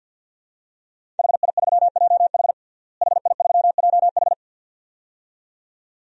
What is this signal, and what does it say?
Morse code, heard as a keyed tone, about 46 wpm, 700 Hz. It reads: HI3JH HI3JH
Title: DXCC-100 Call Signs - 5 Characters - Repeated 2x - 50wpm